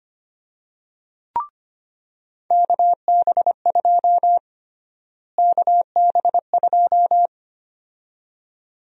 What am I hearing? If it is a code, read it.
Morse code, heard as a keyed tone, about 25 wpm, 700 Hz. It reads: KB2 KB2
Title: USA Call Sign Prefixes - Repeated 2x - 25wpm